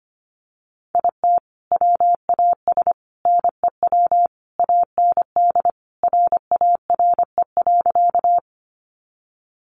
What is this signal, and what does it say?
Morse code, heard as a keyed tone, about 25 wpm, 700 Hz. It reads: IT WAS NEW AND RARE.
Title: Sentences from Top 1100 Words 25wpm